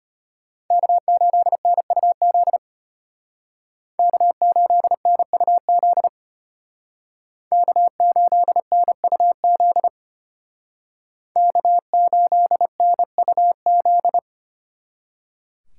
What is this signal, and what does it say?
Morse code, heard as a keyed tone, about 33 wpm, 700 Hz. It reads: K8NUZ K8NUZ K8NUZ K8NUZ